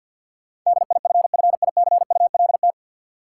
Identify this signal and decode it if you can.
Morse code, heard as a keyed tone, about 50 wpm, 700 Hz. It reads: DIFFICULT
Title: Sets of 4 Words - New Word Drill - 50wpm